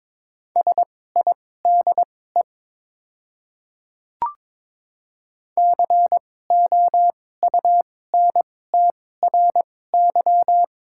Morse code, heard as a keyed tone, about 21 wpm, spaced as for 16 wpm, 700 Hz. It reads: SIDE COUNTRY
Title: 2x Character Spacing - Top 300 Words - 22wpm